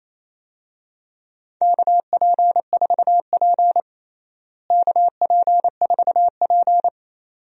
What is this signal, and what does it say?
Morse code, heard as a keyed tone, about 28 wpm, 700 Hz. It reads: KP4P KP4P
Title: DXCC-100 Call Signs - 4 Characters - Repeated 2x - 28wpm